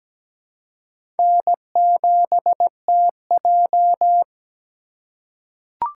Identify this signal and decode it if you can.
Morse code, heard as a keyed tone, about 17 wpm, 700 Hz. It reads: N7TJ
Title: USA Call Signs - 4 Characters - 6x Character Spacing - 17wpm